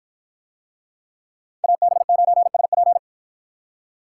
Morse code, heard as a keyed tone, about 53 wpm, 700 Hz. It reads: AB9SP